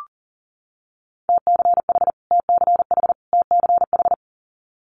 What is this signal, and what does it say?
Morse code, heard as a keyed tone, about 40 wpm, 700 Hz. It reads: TCH TCH TCH
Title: Multi-letter Phonemes Repeated 3x 40wpm